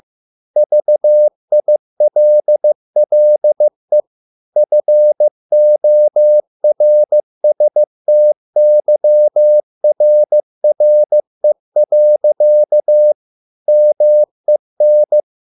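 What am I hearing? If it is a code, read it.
Morse code, heard as a keyed tone, about 15 wpm, 600 Hz. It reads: VILLE FORSTYRRE. MEN
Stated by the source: Morse code at a slow 15 wpm from CW med Gnister.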